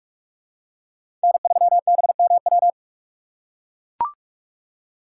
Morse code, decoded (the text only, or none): N3BMW